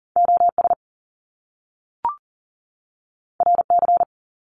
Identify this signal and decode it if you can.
Morse code, heard as a keyed tone, about 40 wpm, 700 Hz. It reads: OS RC
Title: Top Two Letter Combinations in Words 40wpm